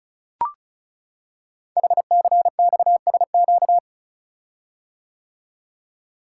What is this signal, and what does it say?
Morse code, heard as a keyed tone, about 36 wpm, 700 Hz. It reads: HCXSQ
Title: Code Groups - 5 Letters - 35wpm